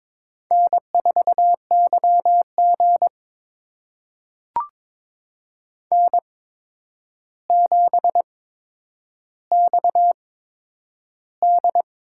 Morse code, heard as a keyed tone, about 22 wpm, spaced as for 9 wpm, 700 Hz. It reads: N4YG N7XD
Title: USA Call Signs - 4 Characters - 8x Character Spacing - 22wpm